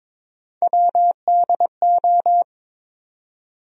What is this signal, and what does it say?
Morse code, heard as a keyed tone, about 22 wpm, 700 Hz. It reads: WDO
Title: USA Call Sign Postfixes - 3 Letters 22wpm